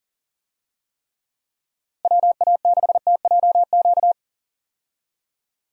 Morse code, heard as a keyed tone, about 37 wpm, 700 Hz. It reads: WA6TJQ
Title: USA Call Signs - 6 Characters - Repeated 3x - 40wpm